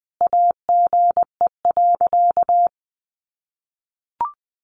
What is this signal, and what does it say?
Morse code, heard as a keyed tone, about 20 wpm, 700 Hz. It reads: AGE.
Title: Sentences from Top 1000 Words 20wpm